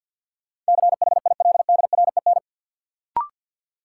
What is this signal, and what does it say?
Morse code, heard as a keyed tone, about 53 wpm, 700 Hz. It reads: CHILDREN
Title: Sentences from Top 300 Words - New Word Drill - 50wpm